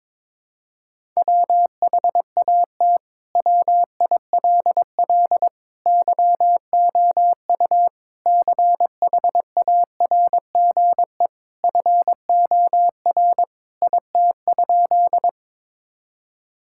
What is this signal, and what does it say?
Morse code, heard as a keyed tone, about 22 wpm, 700 Hz. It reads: WHAT WILL YOU CHARGE FOR IT?
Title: Sentences from Top 1000 Words - Review Part 6 - 22wpm